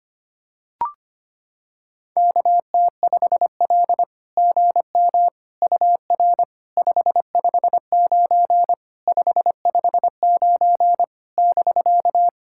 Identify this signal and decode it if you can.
Morse code, heard as a keyed tone, about 25 wpm, 700 Hz. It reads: KT5L GM UR 559 559 <BK>
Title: SOTA QSO Practice 25wpm